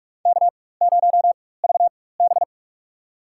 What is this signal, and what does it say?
Morse code, heard as a keyed tone, about 43 wpm, spaced as for 21 wpm, 700 Hz. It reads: K0VB
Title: USA Call Signs - 4 Characters - 4x Character Spacing - 45wpm